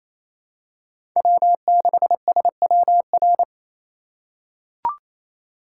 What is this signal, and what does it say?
Morse code, heard as a keyed tone, about 28 wpm, 700 Hz. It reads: W6SWR
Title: USA Call Signs - 5 Characters - 2x Character Spacing - 28wpm